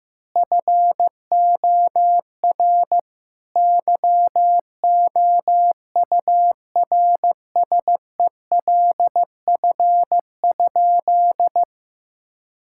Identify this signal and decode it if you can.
Morse code, heard as a keyed tone, about 15 wpm, 700 Hz. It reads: FOR YOURSELF?
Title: Sentences from Top 1000 Words - Review Part 6 - 15wpm